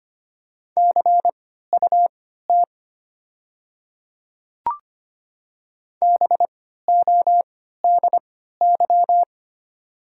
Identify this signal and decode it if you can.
Morse code, heard as a keyed tone, about 25 wpm, spaced as for 14 wpm, 690 Hz. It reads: CUT BODY
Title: 3x Character Spacing - Top 500 Words - 25wpm